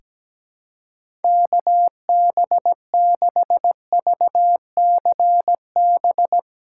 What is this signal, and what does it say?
Morse code, heard as a keyed tone, about 17 wpm, 700 Hz. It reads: KB6VCB
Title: USA Callsign DE Callsign 17wpm